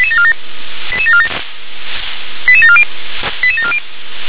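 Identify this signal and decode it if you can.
Five-tone selective call (ZVEI), heard as quick runs of five tones, tones of about 70 ms. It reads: 88747 88740 88740
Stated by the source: ZVEI five-tone call received on SDR.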